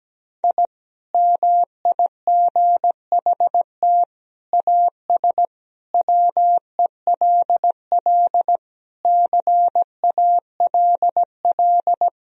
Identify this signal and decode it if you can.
Morse code, heard as a keyed tone, about 17 wpm, 700 Hz. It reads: I MIGHT AS WELL CALL
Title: Sentences from Top 1000 Words - Review Part 2 - 17wpm